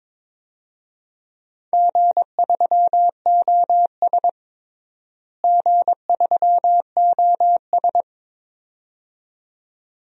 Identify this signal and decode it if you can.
Morse code, heard as a keyed tone, about 21 wpm, 700 Hz. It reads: G3OS G3OS
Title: DXCC-100 Call Signs - 4 Characters - Repeated 2x - 22wpm